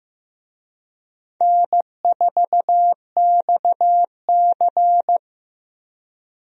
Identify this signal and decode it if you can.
Morse code, heard as a keyed tone, about 15 wpm, 700 Hz. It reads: N4XC
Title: USA Call Signs - 4 Characters - Repeated 3x - 15wpm